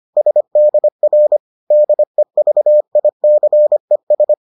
Morse code, heard as a keyed tone, about 25 wpm, 600 Hz. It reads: SDR DEVICES